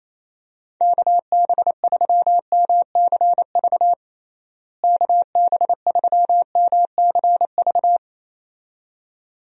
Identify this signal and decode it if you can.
Morse code, heard as a keyed tone, about 28 wpm, 700 Hz. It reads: KB3MCV KB3MCV
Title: USA Call Signs - 6 Characters - Repeated 2x - 28wpm